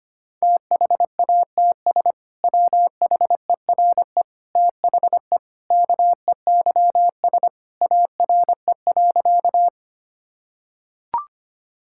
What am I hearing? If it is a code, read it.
Morse code, heard as a keyed tone, about 25 wpm, 700 Hz. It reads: THATS WHERE THE KEYS ARE.
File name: practice / 25wpm / Sentences with 2-Word Phrases - 25wpm